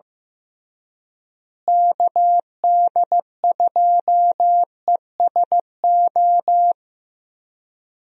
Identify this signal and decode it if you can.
Morse code, heard as a keyed tone, about 16 wpm, 700 Hz. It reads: KD2ESO
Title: USA Call Signs - 6 Characters - Repeated 3x - 15wpm